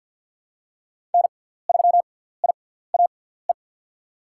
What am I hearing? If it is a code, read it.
Morse code, heard as a keyed tone, about 50 wpm, spaced as for 17 wpm, 690 Hz. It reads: N3IAE
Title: USA Call Signs - 5 Characters - 6x Character Spacing - 50wpm